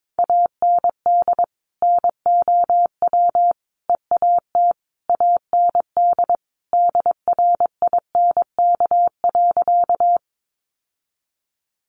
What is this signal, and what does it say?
Morse code, heard as a keyed tone, about 22 wpm, 700 Hz. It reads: AND NOW EAT AND DRINK.
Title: Sentences from Top 1500 Words 22wpm